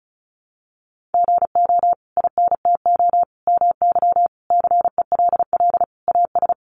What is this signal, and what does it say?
Morse code, heard as a keyed tone, about 34 wpm, 700 Hz. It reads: GO INTO MY CELL AS
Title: Sentences from Top 1100 Words 35wpm